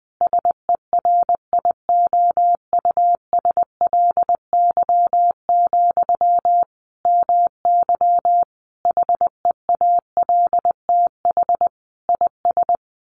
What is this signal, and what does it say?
Morse code, heard as a keyed tone, about 20 wpm, 700 Hz. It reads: SERIOUSLY, MY HEALTH IS
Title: Sentences from Top 1200 Words 20wpm